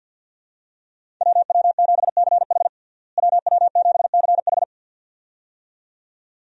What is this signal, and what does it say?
Morse code, heard as a keyed tone, about 46 wpm, 700 Hz. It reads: WW7CH WW7CH
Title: USA Call Signs - 5 Characters - Repeated 2x - 50wpm